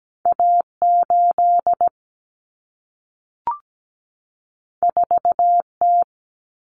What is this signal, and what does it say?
Morse code, heard as a keyed tone, about 17 wpm, 700 Hz. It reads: A8 4T
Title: Common Cut Numbers - 2-Digit 17wpm